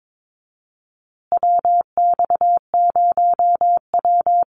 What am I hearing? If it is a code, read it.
Morse code, heard as a keyed tone, about 22 wpm, 700 Hz. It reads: WX0W
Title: USA Call Signs Repeated 2x 22wpm